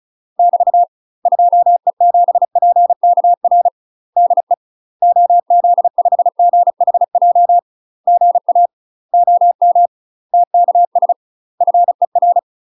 Morse code, heard as a keyed tone, about 34 wpm, 700 Hz. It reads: <BT> 2E7PKR DE OZ5GHJ GA OM TKS FER
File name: UK MASTER QSOs TRAINING / 35WPM / UK MASTER QSOs 0004